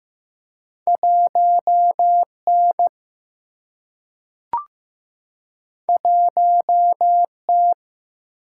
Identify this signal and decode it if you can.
Morse code, heard as a keyed tone, about 15 wpm, 700 Hz. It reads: 1 N 1 T
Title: Single-Letter Single-Number 15wpm